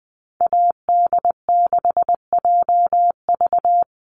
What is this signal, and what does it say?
Morse code, heard as a keyed tone, about 20 wpm, 700 Hz. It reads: AD6JV
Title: Call Signs Repeated 3x - 20wpm